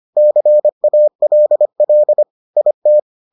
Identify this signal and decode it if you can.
Morse code, heard as a keyed tone, about 25 wpm, 600 Hz. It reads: CALL IT